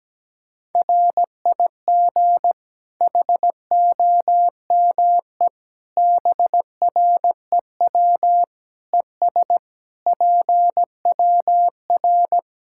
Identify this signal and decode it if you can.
Morse code, heard as a keyed tone, about 17 wpm, 700 Hz. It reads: RIG HOME BREW ES PWR
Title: Practice QSO - Exchange 2 - Info - 17wpm